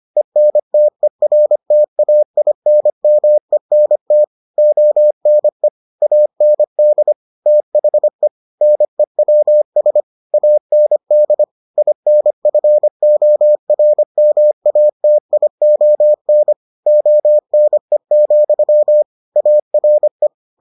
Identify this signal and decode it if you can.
Morse code, heard as a keyed tone, about 25 wpm, 600 Hz. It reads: ENTERTAINMENT ONE AND THE NEWS AND INFORMATION ONE, ARE